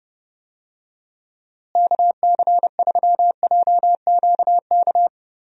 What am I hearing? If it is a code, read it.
Morse code, heard as a keyed tone, about 31 wpm, 700 Hz. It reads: KC3JQK
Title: USA Call Signs - 6 Characters - Repeated 3x - 30wpm